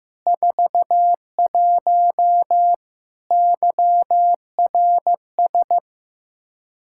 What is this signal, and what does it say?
Morse code, heard as a keyed tone, about 15 wpm, 700 Hz. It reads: 41 YRS
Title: K1USN SST QSO Practice 15wpm